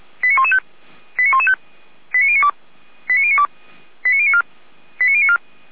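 Five-tone selective call (ZVEI), heard as quick runs of five tones, tones of about 70 ms. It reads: 89115 89115 89092 89092 89094 89094
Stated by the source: ZVEI five-tone call received by SDR.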